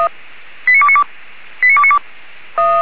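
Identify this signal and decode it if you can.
Five-tone selective call (ZVEI), heard as quick runs of five tones, tones of about 70 ms. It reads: 89292
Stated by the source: Five-tone ZVEI selective call received by SDR, served from audio.f85360.feuerwehr-orlamuende.org (fire brigade Orlamünde).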